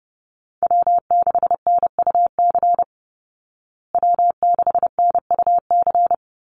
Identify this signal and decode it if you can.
Morse code, heard as a keyed tone, about 30 wpm, 700 Hz. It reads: W6NUC W6NUC
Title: Call Signs Repeated 3x - 30wpm